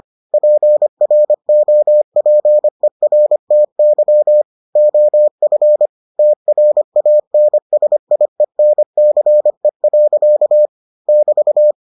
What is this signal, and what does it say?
Morse code, heard as a keyed tone, about 25 wpm, 600 Hz. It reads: PROPERTY OF TRANSIENCE. <BT>